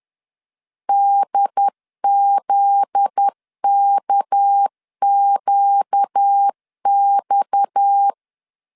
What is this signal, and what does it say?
Morse code, heard as a keyed tone, about 10 wpm, 800 Hz. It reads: DZKQX